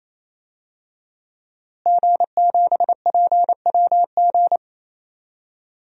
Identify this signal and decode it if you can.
Morse code, heard as a keyed tone, about 28 wpm, 700 Hz. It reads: G7PWG
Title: DXCC-100 Call Signs - 5 Characters - Repeated 3x - 28wpm